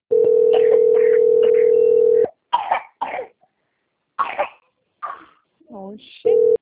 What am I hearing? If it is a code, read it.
Ringback tone (North America), 440 + 480 Hz, with speech.
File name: out- - - - - .42756